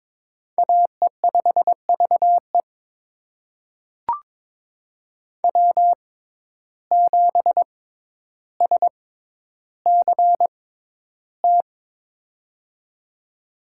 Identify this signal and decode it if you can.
Morse code, heard as a keyed tone, about 22 wpm, spaced as for 11 wpm, 700 Hz. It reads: AE5VE W7SCT
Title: USA Call Signs - 5 Characters - 6x Character Spacing - 22wpm